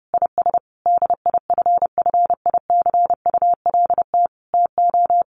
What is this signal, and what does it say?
Morse code, heard as a keyed tone, about 28 wpm, 700 Hz. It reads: IS DIFFICULT TO
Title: Sentences from Top 1500 Words 30wpm